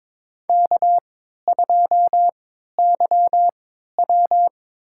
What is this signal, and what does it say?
Morse code, heard as a keyed tone, about 22 wpm, spaced as for 13 wpm, 700 Hz. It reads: K2YW
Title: USA Call Signs - 4 Characters - 3x Character Spacing - 22wpm